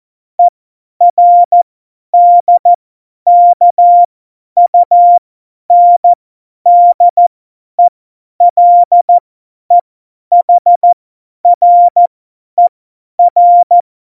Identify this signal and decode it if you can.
Morse code, heard as a keyed tone, about 14 wpm, spaced as for 10 wpm, 700 Hz. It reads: ERDKUNDELEHRER
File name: Morsekurs / Witze / Witz 04. Juni 2019, 50 F70